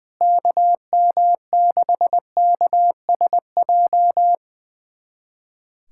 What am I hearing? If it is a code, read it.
Morse code, heard as a keyed tone, about 20 wpm, 700 Hz. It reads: KM6KSJ
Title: USA Callsign DE Callsign - 6 Characters - 20wpm